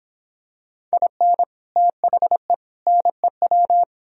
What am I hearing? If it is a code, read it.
Morse code, heard as a keyed tone, about 26 wpm, 700 Hz. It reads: IN THE NEW
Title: Speed-Racing - Sets of 3 Words 22wpm